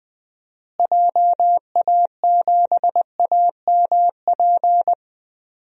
Morse code, heard as a keyed tone, about 20 wpm, 700 Hz. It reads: JA7AMP
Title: DXCC-100 Call Signs - 6 Characters - Repeated 3x - 20wpm